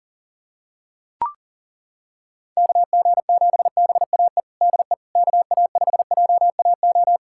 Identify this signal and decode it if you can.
Morse code, heard as a keyed tone, about 40 wpm, 700 Hz. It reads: KG7BAE DE KA5JAO
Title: USA Callsign DE Callsign - 6 Characters - 40wpm